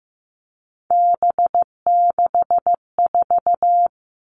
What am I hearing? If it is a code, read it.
Morse code, heard as a keyed tone, about 15 wpm, 700 Hz. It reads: B64
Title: All Cut Numbers - 3-Digit 15wpm